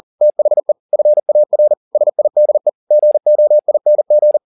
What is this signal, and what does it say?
Morse code, heard as a keyed tone, about 40 wpm, 600 Hz. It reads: THE FAR SIDE GOING
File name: the old man and the sea 2 40wpm